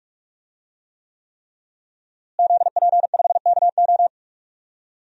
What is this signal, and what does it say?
Morse code, heard as a keyed tone, about 43 wpm, 700 Hz. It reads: ZP5KO
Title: DXCC-100 Call Signs - 5 Characters - Repeated 3x - 45wpm